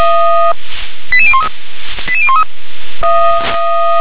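Five-tone selective call (ZVEI), heard as quick runs of five tones, tones of about 70 ms. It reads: 88013 88013
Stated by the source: ZVEI five-tone selective call 88162 received by SDR.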